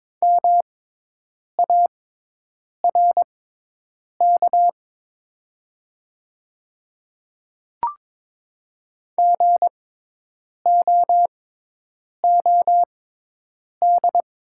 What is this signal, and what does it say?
Morse code, heard as a keyed tone, about 22 wpm, spaced as for 8 wpm, 700 Hz. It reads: MARK GOOD
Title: 6x Character Spacing - Top 300 Words - 22wpm